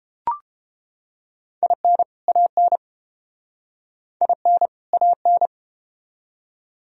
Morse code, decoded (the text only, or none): IN AN IN AN